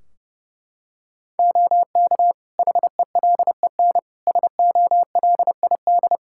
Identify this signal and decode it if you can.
Morse code, heard as a keyed tone, about 30 wpm, 700 Hz. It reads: OK HELEN SOLID
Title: Practice QSO - Exchange 3 - More Info - 30wpm